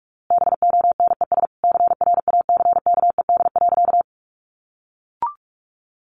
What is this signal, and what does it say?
Morse code, heard as a keyed tone, about 44 wpm, 700 Hz. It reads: BONES CRACKED.
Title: A Sound Of Thunder 45wpm